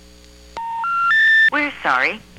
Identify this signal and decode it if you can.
Special information tone, with speech.